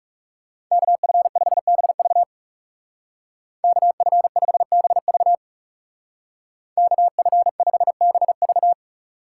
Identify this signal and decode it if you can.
Morse code, heard as a keyed tone, about 39 wpm, 700 Hz. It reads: KF5BV KF5BV KF5BV